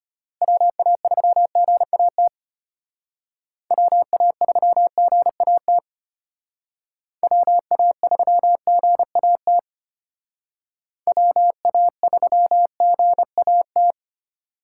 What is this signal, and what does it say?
Morse code, heard as a keyed tone, about 33 wpm, 700 Hz. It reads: WA3GAT WA3GAT WA3GAT WA3GAT